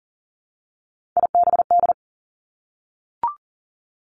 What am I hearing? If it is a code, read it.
Morse code, heard as a keyed tone, about 38 wpm, 700 Hz. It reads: IBD
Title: Three Challenging Letter-Number Combinations 40wpm